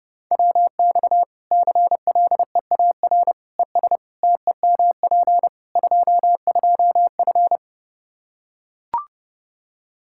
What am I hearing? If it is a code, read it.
Morse code, heard as a keyed tone, about 30 wpm, 700 Hz. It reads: WX CLEAR ES TEMP 22F
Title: Practice QSO - Exchange 2 - Info - 30wpm